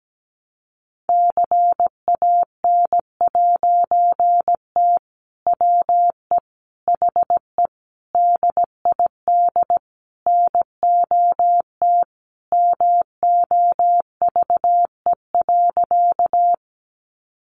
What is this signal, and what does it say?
Morse code, heard as a keyed tone, about 17 wpm, 700 Hz. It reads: CAN'T WE HE DID NOT MOVE.
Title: A Sound Of Thunder 17wpm